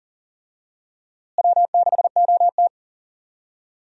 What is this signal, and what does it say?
Morse code, heard as a keyed tone, about 40 wpm, 700 Hz. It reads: W6OT
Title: USA Call Signs - 4 Characters - Repeated 3x - 40wpm